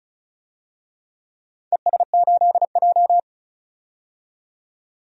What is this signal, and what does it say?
Morse code, heard as a keyed tone, about 36 wpm, 700 Hz. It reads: ES8J
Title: DXCC-100 Call Signs - 4 Characters - 35wpm